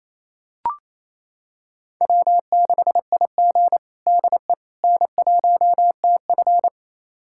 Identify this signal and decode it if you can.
Morse code, heard as a keyed tone, about 28 wpm, 700 Hz. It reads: W6IG DE N1TF
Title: USA Callsign DE Callsign - 4 Characters - 28wpm